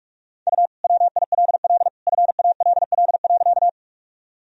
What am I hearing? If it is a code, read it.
Morse code, heard as a keyed tone, about 46 wpm, 700 Hz. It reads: U WILL FALL.
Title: Sentences from Top 500 Words and Abbreviations 45wpm